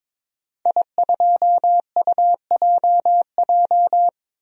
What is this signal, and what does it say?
Morse code, heard as a keyed tone, about 21 wpm, 700 Hz. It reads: I2UJJ
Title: DXCC-100 Call Signs - 5 Characters - 22wpm